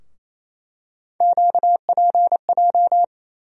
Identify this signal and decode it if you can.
Morse code, heard as a keyed tone, about 27 wpm, 700 Hz. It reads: QPJ
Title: USA Call Sign Postfixes - 3 Letters 28wpm